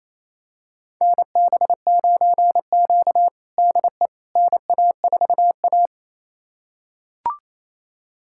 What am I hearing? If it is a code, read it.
Morse code, heard as a keyed tone, about 28 wpm, 700 Hz. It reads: NB9Q DE NA4A
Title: USA Callsign DE Callsign - 4 Characters - 28wpm